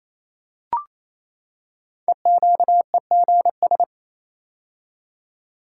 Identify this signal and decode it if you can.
Morse code, heard as a keyed tone, about 29 wpm, 700 Hz. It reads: EQEGS
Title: Code Groups - 5 Letters - 28wpm